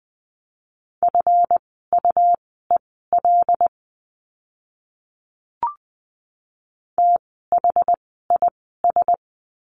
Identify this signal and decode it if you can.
Morse code, heard as a keyed tone, about 20 wpm, spaced as for 15 wpm, 700 Hz. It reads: FUEL THIS
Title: 139 Lesson 23f - Letter f - Words 20wpm